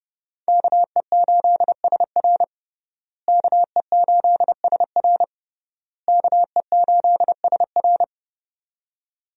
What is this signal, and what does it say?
Morse code, heard as a keyed tone, about 30 wpm, 700 Hz. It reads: KE8SR KE8SR KE8SR
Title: USA Call Signs - 5 Characters - Repeated 3x - 30wpm